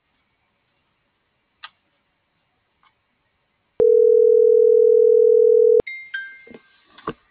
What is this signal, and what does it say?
Ringback tone (North America), 440 + 480 Hz.